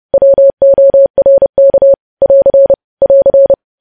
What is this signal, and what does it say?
Morse code, heard as a keyed tone, about 30 wpm, 550 Hz. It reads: WORK <AR> <AR>
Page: news headlines in morse code at 30 wpm